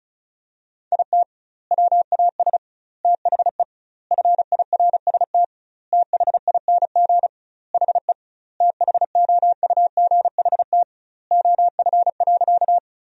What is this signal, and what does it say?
Morse code, heard as a keyed tone, about 36 wpm, 700 Hz. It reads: IT WAS THE FIRST THING HE THOUGHT OF.
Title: 2x Word-Spacing - Sentences from Top 300 Words - 35wpm